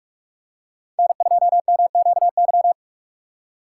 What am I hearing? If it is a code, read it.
Morse code, heard as a keyed tone, about 44 wpm, 700 Hz. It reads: N2MQY
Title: USA Call Signs - 6x Character Spacing - 45wpm